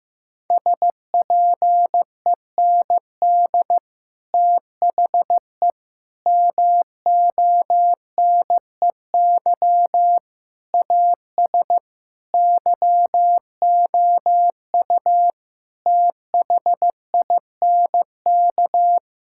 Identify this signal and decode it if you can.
Morse code, heard as a keyed tone, about 15 wpm, 700 Hz. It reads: SPEND THE MONEY AS YOU THINK
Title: Sentences from Top 1000 Words - Review Part 5 - 15wpm